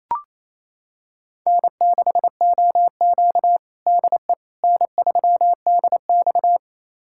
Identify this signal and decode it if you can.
Morse code, heard as a keyed tone, about 28 wpm, 700 Hz. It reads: N6OQ DE N3DX
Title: USA Callsign DE Callsign - 4 Characters - 28wpm